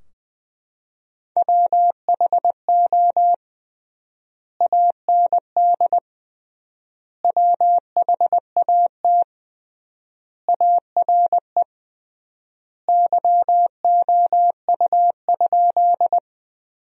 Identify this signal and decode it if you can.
Morse code, heard as a keyed tone, about 21 wpm, 700 Hz. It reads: WHO AND WHAT ARE YOU?
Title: 3x Word-Spacing - Sentences from Top 100 Words - 20wpm